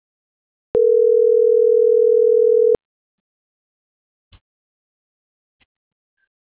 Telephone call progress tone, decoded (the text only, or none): ringback tone